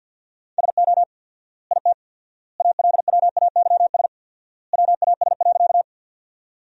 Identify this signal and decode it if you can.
Morse code, heard as a keyed tone, about 50 wpm, 700 Hz. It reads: SO IT ALWAYS WAS.
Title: 4x Word-Spacing - Sentences from Top 300 Words - 50wpm